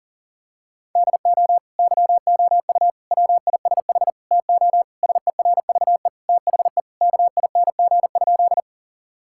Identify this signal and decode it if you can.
Morse code, heard as a keyed tone, about 40 wpm, 700 Hz. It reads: DO YOU WISH TO SERVE THE KING?
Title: Sentences from Top 1600 Words 40wpm